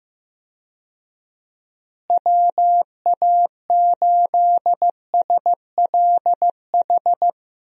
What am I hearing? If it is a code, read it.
Morse code, heard as a keyed tone, about 15 wpm, 700 Hz. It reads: WA8SLH
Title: USA Call Signs - 6 Characters - Repeated 3x - 15wpm